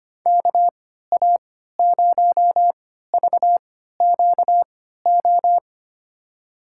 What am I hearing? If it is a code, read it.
Morse code, heard as a keyed tone, about 25 wpm, spaced as for 14 wpm, 700 Hz. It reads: KA0VQO
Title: USA Call Signs - 6 Characters - 3x Character Spacing - 25wpm